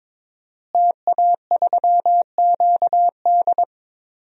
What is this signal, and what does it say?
Morse code, heard as a keyed tone, about 22 wpm, 700 Hz. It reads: TA3QD
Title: DXCC-100 Call Signs - 5 Characters - Repeated 3x - 22wpm